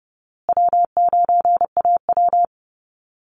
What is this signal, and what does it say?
Morse code, heard as a keyed tone, about 30 wpm, 700 Hz. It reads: W9AW
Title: Easy Call Signs Repeated 3x - 30wpm